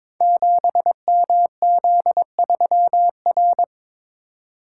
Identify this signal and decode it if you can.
Morse code, heard as a keyed tone, about 22 wpm, 700 Hz. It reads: 7MZ3R
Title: Code Groups - 5 Letter:Numbers - 22wpm